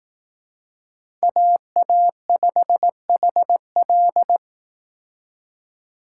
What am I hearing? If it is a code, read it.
Morse code, heard as a keyed tone, about 18 wpm, 700 Hz. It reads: AA5HL
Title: Speed Racing - USA 2x2 Call Signs 15wpm